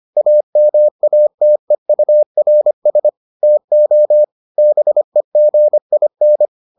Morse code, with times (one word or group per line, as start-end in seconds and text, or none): 0.2-3.1 AMATEURS
3.4-4.2 TO
4.6-6.5 BEGIN